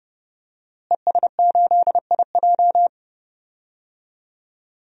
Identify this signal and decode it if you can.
Morse code, heard as a keyed tone, about 31 wpm, 700 Hz. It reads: ES8IJ
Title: DXCC-100 Call Signs - 5 Characters - Repeated 2x - 30wpm